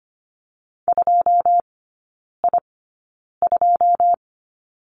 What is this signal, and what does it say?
Morse code, heard as a keyed tone, about 25 wpm, spaced as for 9 wpm, 700 Hz. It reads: 2I2
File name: Lesson 14e - Number 2 - Three Characters 25wpm